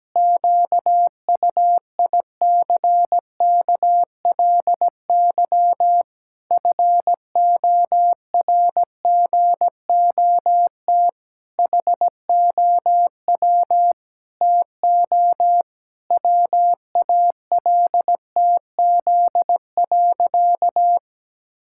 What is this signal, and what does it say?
Morse code, heard as a keyed tone, about 17 wpm, 700 Hz. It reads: QUICKLY FORGOT HOW TO WALTZ.
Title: Pangram Challenge v2 - 17wpm